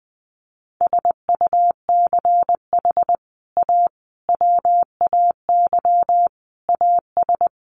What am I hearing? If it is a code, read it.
Morse code, heard as a keyed tone, about 20 wpm, 700 Hz. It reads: SUCH A WAY AS
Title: Sets of 5 Words Encore2 20wpm